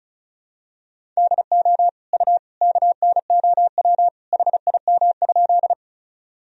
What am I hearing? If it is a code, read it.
Morse code, heard as a keyed tone, about 36 wpm, 700 Hz. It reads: DO U KNOW HIM?
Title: Sentences from Top 100 Words and Abbreviations 35wpm